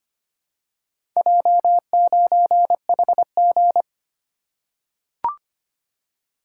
Morse code, heard as a keyed tone, about 25 wpm, 690 Hz. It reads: J9HG
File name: practice / 25wpm / Four Letter-Number Combinations - 25wpm